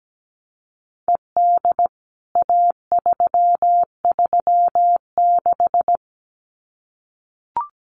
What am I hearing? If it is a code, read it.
Morse code, heard as a keyed tone, about 17 wpm, 700 Hz. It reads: ED A336
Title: CWT Exchange 17wpm